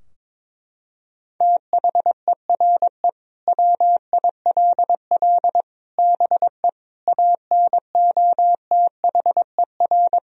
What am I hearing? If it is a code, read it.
Morse code, heard as a keyed tone, about 22 wpm, 700 Hz. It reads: THERE WILL BE ANOTHER
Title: Sentences from Top 1000 Words - Review Part 3 - 22wpm